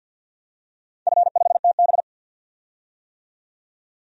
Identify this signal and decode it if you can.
Morse code, heard as a keyed tone, about 47 wpm, 700 Hz. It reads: F5TB
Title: DXCC-100 Call Signs - 4 Characters - 50wpm